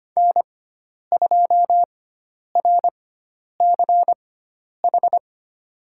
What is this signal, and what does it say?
Morse code, heard as a keyed tone, about 25 wpm, spaced as for 10 wpm, 700 Hz. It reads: N2RCH
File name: course / 25:10wpm / 095 Lesson 16g - Letter c - Call Signs 25wpm